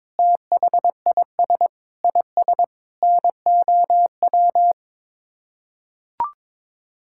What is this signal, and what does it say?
Morse code, heard as a keyed tone, about 22 wpm, 700 Hz. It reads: THIS IS NOW
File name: practice / 22wpm / Sets of 8 Words - 22wpm